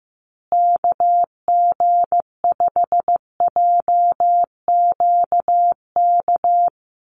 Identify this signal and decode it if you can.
Morse code, heard as a keyed tone, about 15 wpm, 700 Hz. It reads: KG5JQK
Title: USA Call Signs Repeated 2x 15wpm